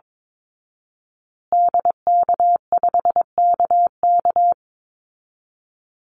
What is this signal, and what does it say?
Morse code, heard as a keyed tone, about 22 wpm, 700 Hz. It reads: DK5KK
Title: Call Signs Repeated 2x - 22wpm